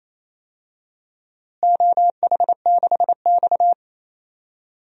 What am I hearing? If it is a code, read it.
Morse code, heard as a keyed tone, about 27 wpm, 700 Hz. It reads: OH6X
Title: DXCC-100 Call Signs - 4 Characters - Repeated 3x - 28wpm